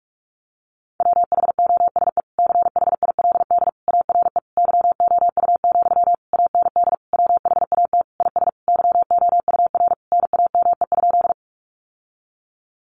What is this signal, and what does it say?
Morse code, heard as a keyed tone, about 44 wpm, 700 Hz. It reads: WHOSE CHILD ARE YOU, AND WHAT IS YOUR NAME?
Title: Sentences from Top 1200 Words 45wpm